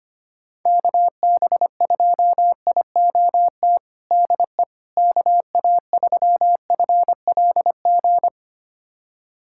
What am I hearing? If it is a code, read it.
Morse code, heard as a keyed tone, about 25 wpm, 700 Hz. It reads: KB2IOT DE KA3FLG
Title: USA Callsign DE Callsign - 6 Characters - 25wpm